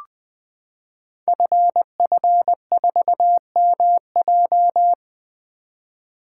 Morse code, heard as a keyed tone, about 20 wpm, 700 Hz. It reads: FF4MJ
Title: Code Groups - 5 Letter:Numbers - 20wpm